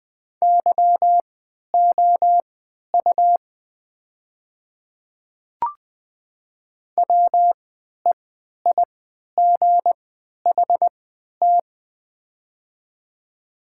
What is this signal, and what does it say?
Morse code, heard as a keyed tone, about 20 wpm, spaced as for 11 wpm, 700 Hz. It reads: YOU WEIGHT
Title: 3x Character Spacing - Top 500 Words - 20wpm